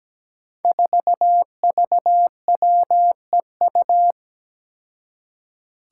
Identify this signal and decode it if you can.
Morse code, heard as a keyed tone, about 17 wpm, 700 Hz. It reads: 4VWEU
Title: Five Challenging Letter-Number Combinations 17wpm